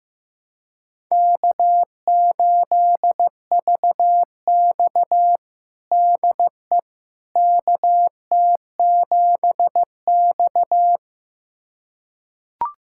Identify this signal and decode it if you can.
Morse code, heard as a keyed tone, about 15 wpm, 700 Hz. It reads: K8VX DE KT7X
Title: USA Callsign DE Callsign - 4 Characters - 15wpm